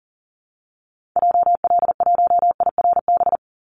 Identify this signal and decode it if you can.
Morse code, heard as a keyed tone, about 40 wpm, 700 Hz. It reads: JL1IRB
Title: Call Signs Repeated 3x - 40wpm